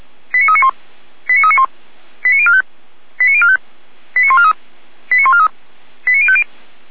Five-tone selective call (ZVEI), heard as quick runs of five tones, tones of about 70 ms. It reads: 89391 89391 89045 89045 89143 89143 89050
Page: SDR/ZVEI